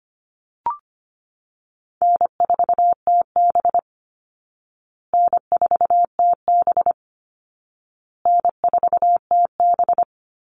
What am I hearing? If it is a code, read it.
Morse code, heard as a keyed tone, about 25 wpm, 700 Hz. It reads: N4TB N4TB N4TB
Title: Call Signs Repeated 2x - 25wpm